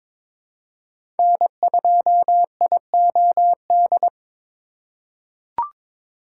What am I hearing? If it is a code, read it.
Morse code, heard as a keyed tone, about 22 wpm, 700 Hz. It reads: N2IOD
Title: USA Call Signs - 5 Characters - 2x Character Spacing - 22wpm